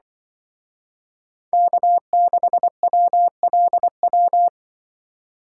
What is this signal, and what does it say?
Morse code, heard as a keyed tone, about 23 wpm, 700 Hz. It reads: K6WLW
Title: Speed Racing - USA Call Signs 20wpm